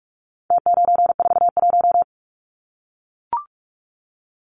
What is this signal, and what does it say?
Morse code, heard as a keyed tone, about 44 wpm, 700 Hz. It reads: T941
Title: Numbers - Contest Style - 4-Digit 45wpm